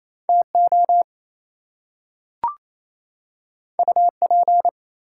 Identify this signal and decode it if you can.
Morse code, heard as a keyed tone, about 28 wpm, 700 Hz. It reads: TO UP
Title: Two Letter Words - 28wpm